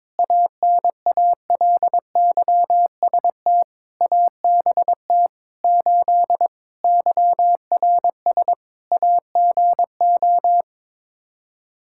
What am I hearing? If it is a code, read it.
Morse code, heard as a keyed tone, about 21 wpm, 700 Hz. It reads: ANALYST ABT 8 YRS AGO
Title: Practice QSO - Exchange 3 - More Info - 22wpm